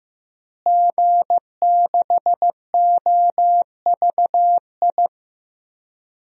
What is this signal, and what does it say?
Morse code, heard as a keyed tone, about 15 wpm, 700 Hz. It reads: G6OVI
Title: DXCC-100 Call Signs - 5 Characters - Repeated 2x - 15wpm